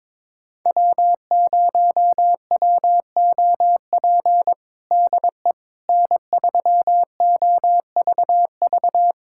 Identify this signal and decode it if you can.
Morse code, heard as a keyed tone, about 22 wpm, 700 Hz. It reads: W0WOP DE N3OVV
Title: USA Callsign DE Callsign - 5 Characters - 22wpm